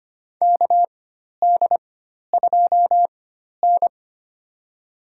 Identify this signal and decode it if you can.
Morse code, heard as a keyed tone, about 25 wpm, spaced as for 12 wpm, 700 Hz. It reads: KD2N
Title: USA Call Signs - 4 Characters - 4x Character Spacing - 25wpm